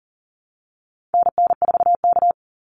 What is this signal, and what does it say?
Morse code, heard as a keyed tone, about 40 wpm, 700 Hz. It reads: NN4K
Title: Call Signs - 40wpm